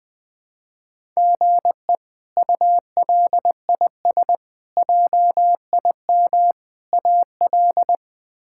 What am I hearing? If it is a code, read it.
Morse code, heard as a keyed tone, about 21 wpm, 700 Hz. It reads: GE ULIS JIM AL
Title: K1USN SST QSO Practice - 20wpm